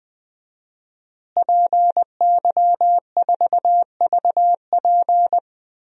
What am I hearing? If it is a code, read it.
Morse code, heard as a keyed tone, about 20 wpm, 700 Hz. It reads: PY4VP
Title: DXCC-100 Call Signs - 5 Characters - Repeated 3x - 20wpm